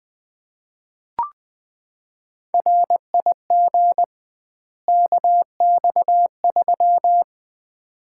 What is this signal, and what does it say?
Morse code, heard as a keyed tone, about 20 wpm, 700 Hz. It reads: RIG KX3
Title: Common QSO Phrases - 2x Word-Spacing - 20wpm